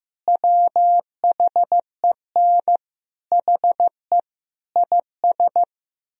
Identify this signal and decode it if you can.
Morse code, heard as a keyed tone, about 15 wpm, 700 Hz. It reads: WHEN HE IS
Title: Sentences from Top 1000 Words - Review Part 4 - 15wpm